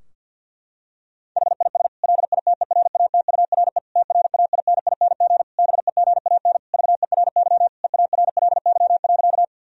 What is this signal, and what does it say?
Morse code, heard as a keyed tone, about 50 wpm, 700 Hz. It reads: HIS LITERATURE TRAINING BEGAN VERY EARLY.
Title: Sentences from Top 1000 Words - Review Part 4 - 50wpm